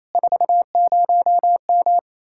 Morse code, heard as a keyed tone, about 38 wpm, 700 Hz. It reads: 40M